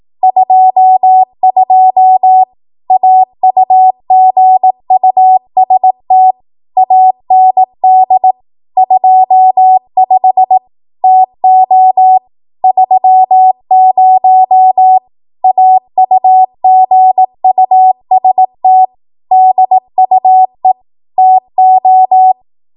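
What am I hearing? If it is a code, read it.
Morse code, heard as a keyed tone, about 18 wpm, 750 Hz. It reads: 22 AUGUST AND 25 TO 30 AUGUST DUE TO
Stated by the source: ARRL 18 wpm Morse code practice archive transmission.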